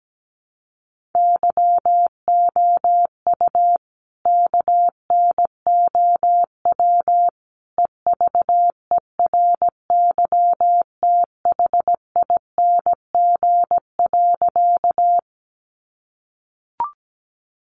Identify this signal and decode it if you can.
Morse code, heard as a keyed tone, about 17 wpm, 700 Hz. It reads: YOU KNOW EVERYTHING.
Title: Sentences from Top 700 Words 17wpm